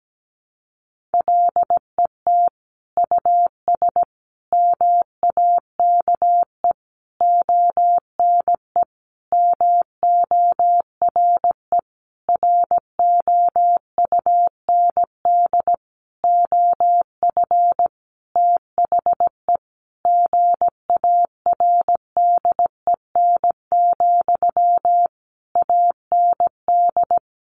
Morse code, heard as a keyed tone, about 17 wpm, 700 Hz. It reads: LET US MAKE ONE MORE ROUND OF THE GARDEN, AND
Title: Sentences from Top 700 Words 17wpm